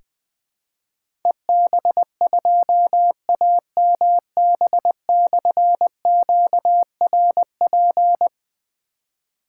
Morse code, heard as a keyed tone, about 21 wpm, 700 Hz. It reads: EB2AMB/QRP
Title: Difficult Call Signs 20wpm